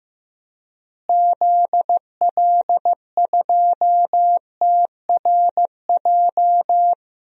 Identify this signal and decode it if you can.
Morse code, heard as a keyed tone, about 15 wpm, 700 Hz. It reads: ZL2TRJ
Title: DXCC-100 Call Signs - 6 Characters - Repeated 2x - 15wpm